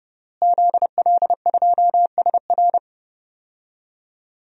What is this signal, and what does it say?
Morse code, heard as a keyed tone, about 32 wpm, 700 Hz. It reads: ZL2SR
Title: DXCC-100 Call Signs - 5 Characters - Repeated 3x - 30wpm